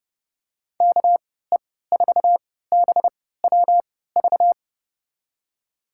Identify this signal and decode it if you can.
Morse code, heard as a keyed tone, about 30 wpm, spaced as for 17 wpm, 700 Hz. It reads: KE4BWV
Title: USA Call Signs - 6 Characters - 3x Character Spacing - 30wpm